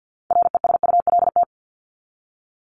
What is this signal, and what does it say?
Morse code, heard as a keyed tone, about 50 wpm, 700 Hz. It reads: RESULT
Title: Top 501-750 Words 50wpm